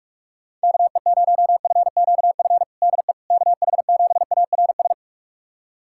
Morse code, heard as a keyed tone, about 46 wpm, 690 Hz. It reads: KE0FQF DE KH7ARS
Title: USA Callsign DE Callsign - 6 Characters - 45wpm